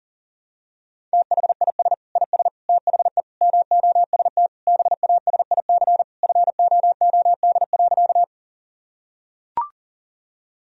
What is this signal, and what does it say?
Morse code, heard as a keyed tone, about 40 wpm, 700 Hz. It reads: THIS IS THE MOST BASIC FOOD.
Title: Sentences from Top 1000 Words - Review Part 3 - 40wpm